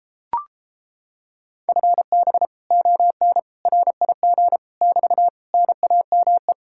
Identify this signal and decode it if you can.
Morse code, heard as a keyed tone, about 31 wpm, 700 Hz. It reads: FB ON RIG <BT> NAME